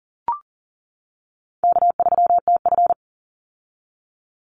Morse code, heard as a keyed tone, about 40 wpm, 700 Hz. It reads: K3TF
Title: Easy Call Signs - 40wpm